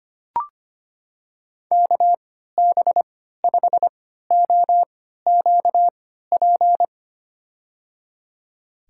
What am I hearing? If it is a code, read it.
Morse code, heard as a keyed tone, about 25 wpm, spaced as for 14 wpm, 700 Hz. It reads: KB5OQP
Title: USA Call Signs - 3x Character Spacing - 25wpm